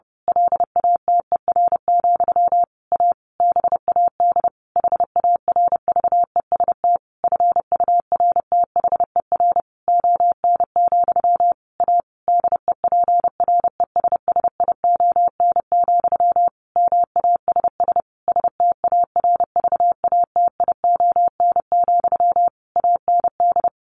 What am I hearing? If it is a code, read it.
Morse code, heard as a keyed tone, about 30 wpm, 700 Hz. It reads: LATER, A BAD HARVEST FURTHER ON, A DEPRESSION, MASS STARVATION, AND
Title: A Sound Of Thunder 30wpm